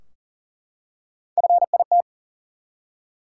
Morse code, heard as a keyed tone, about 40 wpm, 700 Hz. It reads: FIT
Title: Three Letter Words - 40wpm